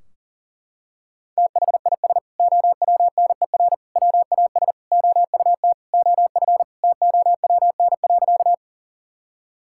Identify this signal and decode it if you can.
Morse code, heard as a keyed tone, about 40 wpm, 700 Hz. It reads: THIS OWNER WAS OUT OF TOWN.
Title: Sentences from Top 1600 Words 40wpm